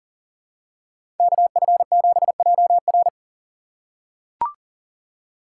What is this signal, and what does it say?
Morse code, heard as a keyed tone, about 40 wpm, 700 Hz. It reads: KF7JR
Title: USA Call Signs - 2x Character Spacing - 40wpm